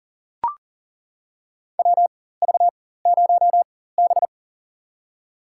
Morse code, heard as a keyed tone, about 40 wpm, spaced as for 19 wpm, 700 Hz. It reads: WV0B
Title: USA Call Signs - 4 Characters - 4x Character Spacing - 40wpm